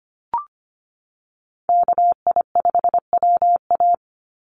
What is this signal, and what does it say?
Morse code, heard as a keyed tone, about 25 wpm, 700 Hz. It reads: KI5WA
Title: NA SOTA Chaser Callsigns 25wpm